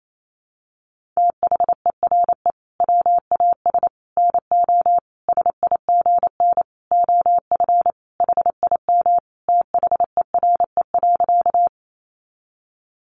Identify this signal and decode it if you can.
Morse code, heard as a keyed tone, about 28 wpm, 700 Hz. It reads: THERE WAS NO SIGN OF HIM THERE.
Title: Sentences from Top 1000 Words 28wpm